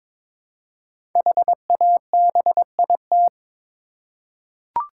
Morse code, heard as a keyed tone, about 23 wpm, 700 Hz. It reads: HABIT
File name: 5 Letter Words - First 3 Letters Given - 22wpm